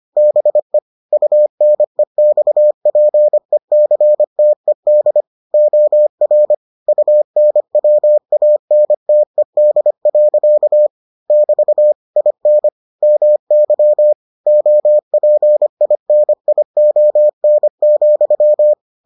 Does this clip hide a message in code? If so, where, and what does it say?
Yes: Morse code from 0.2 to 18.7 s, heard as a keyed tone, about 25 wpm, 600 Hz. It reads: BE UNEXPECTED OR UNWANTED. <BT> IN MY OPINION,